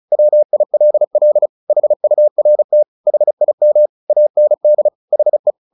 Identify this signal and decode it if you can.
Morse code, heard as a keyed tone, about 34 wpm, 600 Hz. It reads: WILL HURT HIM AND HE
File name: the old man and the sea / the old man and the sea 1 35wpm